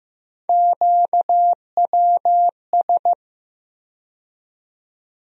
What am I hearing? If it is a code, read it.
Morse code, heard as a keyed tone, about 16 wpm, 700 Hz. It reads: QWS
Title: USA Call Sign Postfixes - 3 Letters 15wpm